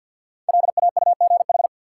Morse code, heard as a keyed tone, about 52 wpm, 700 Hz. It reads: LAUGH
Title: Five Letter Words - 50wpm